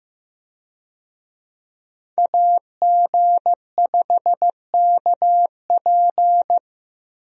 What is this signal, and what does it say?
Morse code, heard as a keyed tone, about 15 wpm, 700 Hz. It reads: AG5KP
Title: USA Call Signs - 5 Characters - Repeated 2x - 15wpm